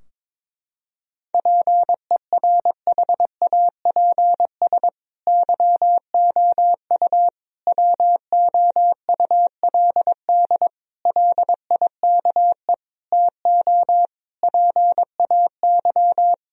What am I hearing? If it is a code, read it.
Morse code, heard as a keyed tone, about 22 wpm, 700 Hz. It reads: PERHAPS YOU WOULD LIKE TO PAY
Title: Sentences from Top 1000 Words - Review Part 2 - 22wpm